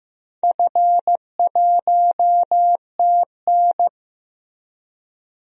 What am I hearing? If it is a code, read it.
Morse code, heard as a keyed tone, about 16 wpm, 700 Hz. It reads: F1TN
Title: DXCC-100 Call Signs - 4 Characters - Repeated 2x - 15wpm